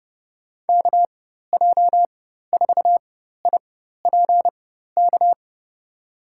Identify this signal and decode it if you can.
Morse code, heard as a keyed tone, about 30 wpm, spaced as for 14 wpm, 700 Hz. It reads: KJ4IPK
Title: USA Call Signs - 6 Characters - 4x Character Spacing - 30wpm